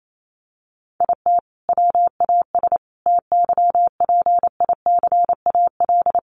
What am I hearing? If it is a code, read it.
Morse code, heard as a keyed tone, about 28 wpm, 700 Hz. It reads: IT WAS TYPICAL
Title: Sentences from Top 1100 Words 28wpm